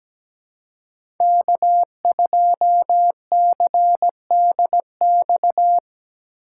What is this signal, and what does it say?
Morse code, heard as a keyed tone, about 17 wpm, 700 Hz. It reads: K2CDX
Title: K1USN SST QSO Practice - 17wpm